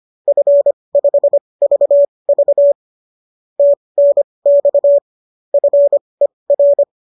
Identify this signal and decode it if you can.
Morse code, heard as a keyed tone, about 25 wpm, spaced as for 20 wpm, 570 Hz. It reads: F5VV TNX FER